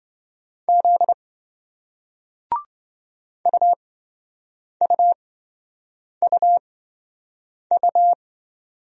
Morse code, heard as a keyed tone, about 26 wpm, spaced as for 7 wpm, 700 Hz. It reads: Z UUUU